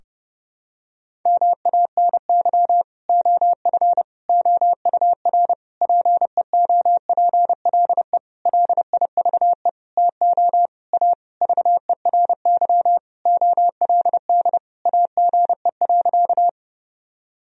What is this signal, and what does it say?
Morse code, heard as a keyed tone, about 32 wpm, 700 Hz. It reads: MANY OF OUR PEOPLE LIVE TO A VERY OLD AGE.
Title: Sentences from Top 1000 Words - Review Part 3 - 30wpm